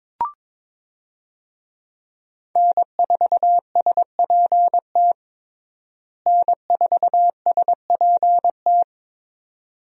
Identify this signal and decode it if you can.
Morse code, heard as a keyed tone, about 21 wpm, 700 Hz. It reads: N4SPT N4SPT
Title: USA Call Signs - 5 Characters - Repeated 2x - 22wpm